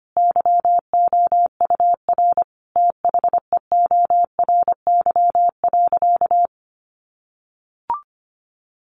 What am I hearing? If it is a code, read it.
Morse code, heard as a keyed tone, about 25 wpm, 700 Hz. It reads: YOUR THEORY.
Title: Sentences from Top 800 Words 25wpm